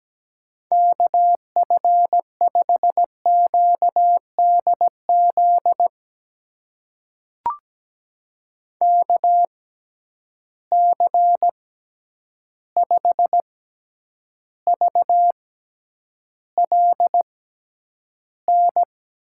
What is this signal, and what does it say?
Morse code, heard as a keyed tone, about 17 wpm, spaced as for 9 wpm, 700 Hz. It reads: KF5QDZ KC5VLN
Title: USA Call Signs - 6 Characters - 6x Character Spacing - 17wpm